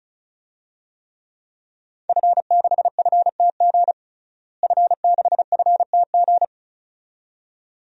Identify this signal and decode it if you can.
Morse code, heard as a keyed tone, about 36 wpm, 700 Hz. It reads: F6FTG F6FTG
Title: DXCC-100 Call Signs - 5 Characters - Repeated 2x - 35wpm